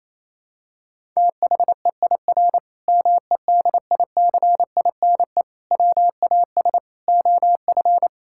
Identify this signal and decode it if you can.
Morse code, heard as a keyed tone, about 27 wpm, 700 Hz. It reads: THEIR MEDICINE WAS OF
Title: Sentences from Top 1000 Words - Review Part 6 - 28wpm